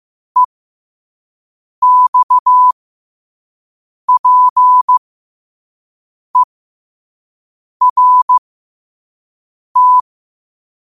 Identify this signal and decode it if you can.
Morse code, heard as a keyed tone, about 15 wpm, spaced as for 5 wpm, 1000 Hz. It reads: EXPERT
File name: qotdfiles / 5wpm